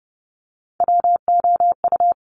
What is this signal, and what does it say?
Morse code, heard as a keyed tone, about 30 wpm, 690 Hz. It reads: WOU
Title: Top Three Letter Combinations in Words 30wpm